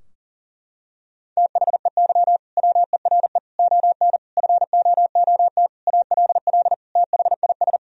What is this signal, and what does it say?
Morse code, heard as a keyed tone, about 40 wpm, 700 Hz. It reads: THEY WERE ON FOOT ALL THIS
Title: Sentences from Top 1000 Words - Review Part 3 - 40wpm